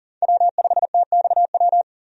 Morse code, heard as a keyed tone, about 40 wpm, 700 Hz. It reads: W5TXW
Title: USA Call Signs - 5 Characters - 40wpm